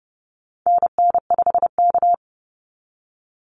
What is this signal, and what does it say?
Morse code, heard as a keyed tone, about 30 wpm, 700 Hz. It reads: NN5K
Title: NA SOTA Activator Callsigns 30wpm